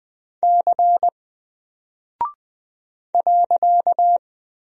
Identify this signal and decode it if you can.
Morse code, heard as a keyed tone, about 20 wpm, 700 Hz. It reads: C .
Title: Single Character (Letter:Number:Punctuation:Prosign) 20wpm